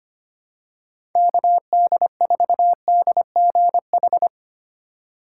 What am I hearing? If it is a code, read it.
Morse code, heard as a keyed tone, about 25 wpm, 700 Hz. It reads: KD4DGH
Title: USA Call Signs - 6 Characters - 4x Character Spacing - 25wpm